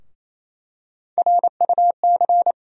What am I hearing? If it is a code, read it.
Morse code, heard as a keyed tone, about 28 wpm, 700 Hz. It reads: RUC
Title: USA Call Sign Postfixes - 3 Letters 28wpm